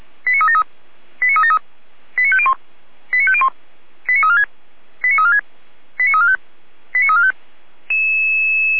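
Five-tone selective call (ZVEI), heard as quick runs of five tones, tones of about 70 ms. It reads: 89373 89373 89501 89501 89346 89346 89345 89345
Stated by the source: ZVEI five-tone call received by SDR, from the feuerwehr-orlamuende.org (fire brigade Orlamünde) server.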